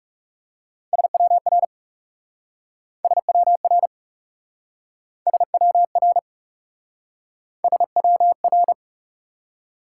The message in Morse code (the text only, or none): SWR SWR SWR SWR